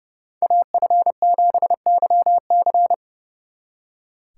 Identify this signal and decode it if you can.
Morse code, heard as a keyed tone, about 39 wpm, 700 Hz. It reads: AF7YC